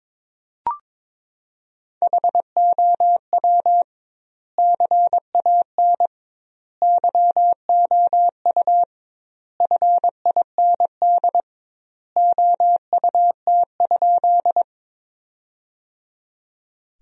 Morse code, heard as a keyed tone, about 22 wpm, 700 Hz. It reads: HOW CAN YOU FIND OUT?
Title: 2x Word-Spacing - Sentences from Top 200 Words - 22wpm